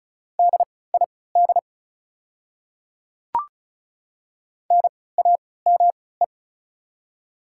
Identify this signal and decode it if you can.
Morse code, heard as a keyed tone, about 36 wpm, spaced as for 20 wpm, 700 Hz. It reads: DID NAME
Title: 3x Character Spacing - Top 300 Words - 35wpm